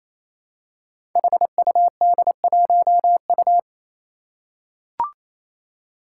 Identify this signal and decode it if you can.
Morse code, heard as a keyed tone, about 27 wpm, 700 Hz. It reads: HUD1U
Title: Five Challenging Letter-Number Combinations 28wpm